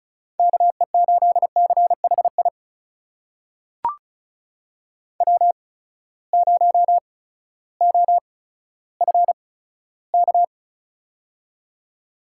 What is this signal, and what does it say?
Morse code, heard as a keyed tone, about 34 wpm, spaced as for 16 wpm, 700 Hz. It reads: KE8CHI W0OFK
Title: USA Call Signs - 8x Character Spacing - 35wpm